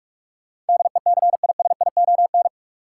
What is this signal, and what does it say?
Morse code, heard as a keyed tone, about 42 wpm, 700 Hz. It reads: DECISION
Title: Sentences from Top 1000 Words - New Word Drill - 45wpm